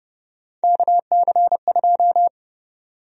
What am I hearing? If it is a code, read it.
Morse code, heard as a keyed tone, about 30 wpm, 700 Hz. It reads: KC2
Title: USA Call Sign Prefixes - 30wpm